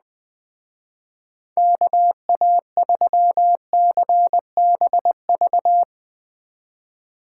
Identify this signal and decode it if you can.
Morse code, heard as a keyed tone, about 20 wpm, 700 Hz. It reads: KA3CBV
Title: Speed Racing - USA 2x3 Call Signs 17wpm